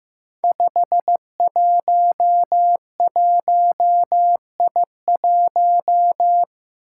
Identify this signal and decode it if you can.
Morse code, heard as a keyed tone, about 15 wpm, 700 Hz. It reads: 511I1
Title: Five Challenging Letter-Number Combinations 15wpm